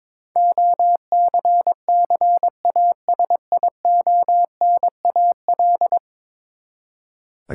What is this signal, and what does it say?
Morse code, heard as a keyed tone, about 21 wpm, 700 Hz. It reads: OCCASIONAL
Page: morse-code-ninja-practice-22wpm